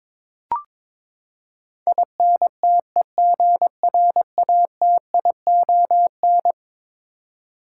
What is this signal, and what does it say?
Morse code, heard as a keyed tone, about 22 wpm, 700 Hz. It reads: INTEGRATION
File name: practice / 22wpm / Eleven Letter Words - 22wpm